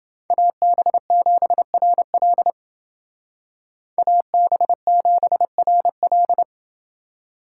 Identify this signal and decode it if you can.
Morse code, heard as a keyed tone, about 28 wpm, 700 Hz. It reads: AB7RL AB7RL